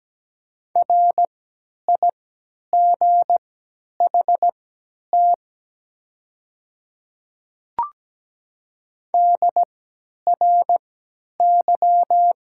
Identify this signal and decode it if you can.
Morse code, heard as a keyed tone, about 17 wpm, spaced as for 10 wpm, 700 Hz. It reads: RIGHT DRY
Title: 3x Character Spacing - Top 500 Words - 17wpm